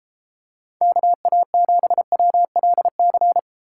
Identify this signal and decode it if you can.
Morse code, heard as a keyed tone, about 32 wpm, 700 Hz. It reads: KA7WLC